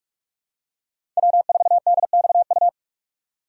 Morse code, heard as a keyed tone, about 46 wpm, 700 Hz. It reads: W4DXU
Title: USA Call Signs - 5 Characters - 45wpm